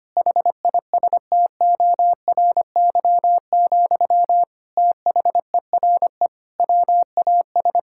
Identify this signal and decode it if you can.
Morse code, heard as a keyed tone, about 25 wpm, 700 Hz. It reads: HISTORY, THERE WAS
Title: Sentences from Top 1000 Words - Review Part 5 - 25wpm